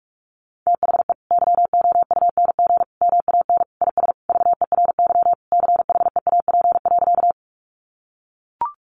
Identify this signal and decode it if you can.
Morse code, heard as a keyed tone, about 44 wpm, 700 Hz. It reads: THE YOUNG MAN IS VERY CHEAP.
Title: Sentences from Top 1200 Words 45wpm